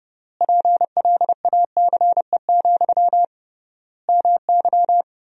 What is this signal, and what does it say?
Morse code, heard as a keyed tone, about 30 wpm, 700 Hz. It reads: PLACE, MY
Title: 3x Word-Spacing - Sentences from Top 400 Words - 30wpm